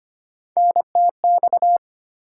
Morse code, heard as a keyed tone, about 25 wpm, 700 Hz. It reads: NTX
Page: morse-code-ninja-practice-25wpm